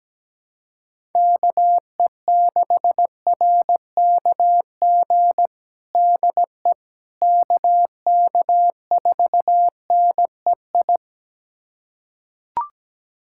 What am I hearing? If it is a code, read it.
Morse code, heard as a keyed tone, about 17 wpm, 700 Hz. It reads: KE6RKG DE KK4NEI
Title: USA Callsign DE Callsign 17wpm